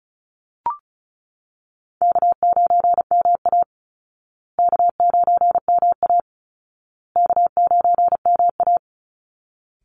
Morse code, heard as a keyed tone, about 34 wpm, 700 Hz. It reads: K9MA K9MA K9MA
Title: Easy Call Signs Repeated 3x - 35wpm